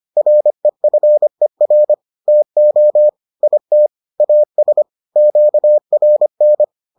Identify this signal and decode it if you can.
Morse code, heard as a keyed tone, about 25 wpm, 600 Hz. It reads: REFER TO IT AS QRN